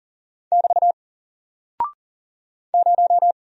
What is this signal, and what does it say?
Morse code, heard as a keyed tone, about 40 wpm, 700 Hz. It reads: <BT> 0
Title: Single Character (Letter:Number:Punctuation:Prosign) 40wpm